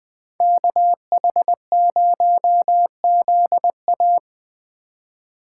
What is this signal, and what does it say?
Morse code, heard as a keyed tone, about 20 wpm, 700 Hz. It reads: KH0ZA